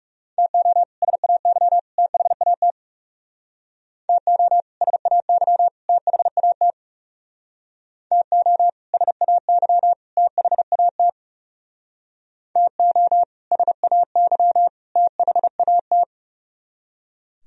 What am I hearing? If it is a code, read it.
Morse code, heard as a keyed tone, about 36 wpm, 700 Hz. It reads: TO SAY THAT TO SAY THAT TO SAY THAT TO SAY THAT
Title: Speed-Racing - Sets of 3 Words 30wpm